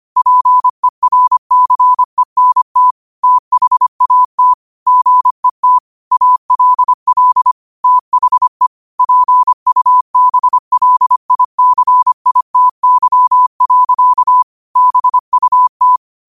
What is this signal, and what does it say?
Morse code, heard as a keyed tone, about 25 wpm, 1000 Hz. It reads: PERCENT THAT GET ALL THE PUBLICITY. BUT